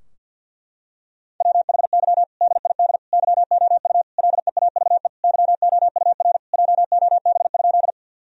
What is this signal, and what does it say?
Morse code, heard as a keyed tone, about 51 wpm, 700 Hz. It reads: WHY DID YOU LEAVE YOUR JOB?
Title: Sentences from Top 1000 Words - Review Part 3 - 50wpm